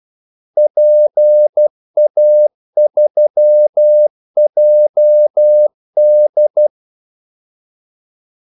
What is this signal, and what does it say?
Morse code, heard as a keyed tone, about 12 wpm, 600 Hz. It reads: PA3JD
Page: 2 ekstra ordmellemrum mellem kaldesignalerne